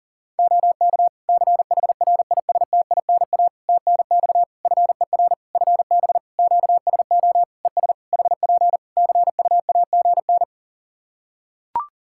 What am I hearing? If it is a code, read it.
Morse code, heard as a keyed tone, about 40 wpm, 700 Hz. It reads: OK CHRISTINA TNX FER FB QSO ES HP CUAGN
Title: Practice QSO - Exchange 4 - Ending - 40wpm